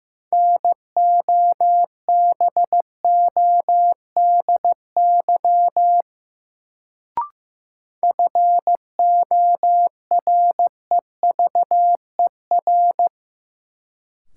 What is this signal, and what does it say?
Morse code, heard as a keyed tone, about 15 wpm, 700 Hz. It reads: NOBODY FOREVER
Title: Sentences from Top 1600 Words - New Word Drill - 15wpm